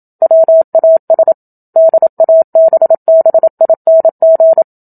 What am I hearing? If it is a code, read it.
Morse code, heard as a keyed tone, about 27 wpm, 650 Hz. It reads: WAS DABBING
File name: Book Two / 01-Chapter 1 - 27 WPM